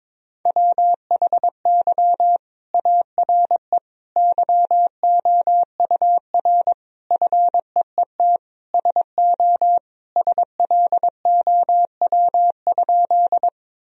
Morse code, heard as a keyed tone, about 22 wpm, 700 Hz. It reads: WHY ARE YOUR FEET SO SLOW?
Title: Sentences from Top 1000 Words - Review Part 2 - 22wpm